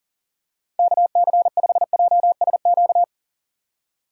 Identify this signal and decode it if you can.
Morse code, heard as a keyed tone, about 40 wpm, 700 Hz. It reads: KC5JSQ
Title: USA Call Signs - 6 Characters - 4x Character Spacing - 40wpm